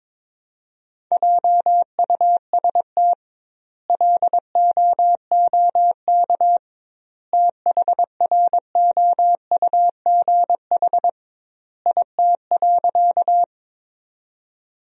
Morse code, heard as a keyed tone, about 22 wpm, 700 Hz. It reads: JUST LOOK THROUGH IT.
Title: 2x Word-Spacing - Sentences from Top 200 Words - 22wpm